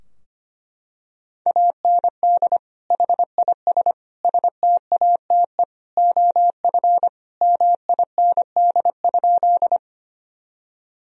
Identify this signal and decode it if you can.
Morse code, heard as a keyed tone, about 25 wpm, 700 Hz. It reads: AND HIS STATE OF MIND?
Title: Sentences from Top 1000 Words - Review Part 2 - 25wpm